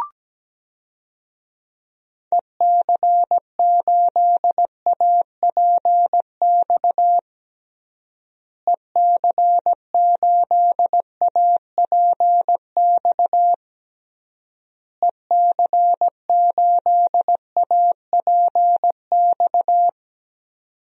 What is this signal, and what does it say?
Morse code, heard as a keyed tone, about 17 wpm, 700 Hz. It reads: EC8APX EC8APX EC8APX
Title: DXCC-100 Call Signs - 6 Characters - Repeated 3x - 17wpm